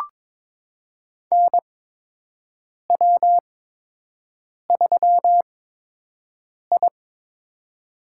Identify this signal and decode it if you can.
Morse code, heard as a keyed tone, about 22 wpm, spaced as for 6 wpm, 700 Hz. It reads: NW3I
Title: USA Call Signs - 4 Characters - 8x Character Spacing - 22wpm